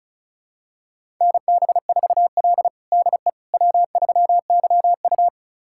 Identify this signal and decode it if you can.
Morse code, heard as a keyed tone, about 36 wpm, 700 Hz. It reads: NB4L DE W3YU
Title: USA Callsign DE Callsign - 4 Characters - 35wpm